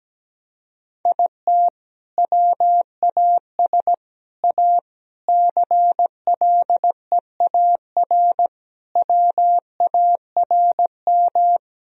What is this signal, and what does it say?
Morse code, heard as a keyed tone, about 17 wpm, 700 Hz. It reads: IT WAS A CLEAR WARM
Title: Sentences from Top 1000 Words - Review Part 2 - 17wpm